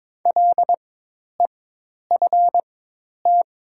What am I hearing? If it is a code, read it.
Morse code, heard as a keyed tone, about 22 wpm, spaced as for 10 wpm, 700 Hz. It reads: LEFT
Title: 4x Character Spacing - Top 500 Words - 22wpm